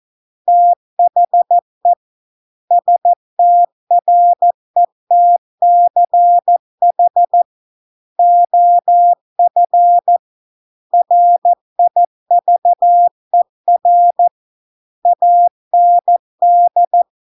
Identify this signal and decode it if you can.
Morse code, heard as a keyed tone, about 14 wpm, 700 Hz. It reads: THE STRETCH OF RIVER AND